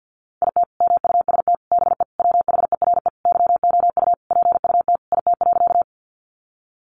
Morse code, heard as a keyed tone, about 50 wpm, 700 Hz. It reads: IT MUST BE WHERE YOU PUT IT.
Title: Sentences from Top 200 Words 50wpm